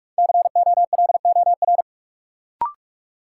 Morse code, heard as a keyed tone, about 46 wpm, 700 Hz. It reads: COLOR
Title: Sentences from Top 1000 Words - Review Part 2 - New Word Drill - 45wpm